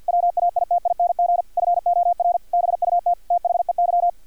Morse code, heard as a keyed tone, about 50 wpm, 700 Hz, with radio noise. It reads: WAITING FOR BUT THEY
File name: archived / CRSnet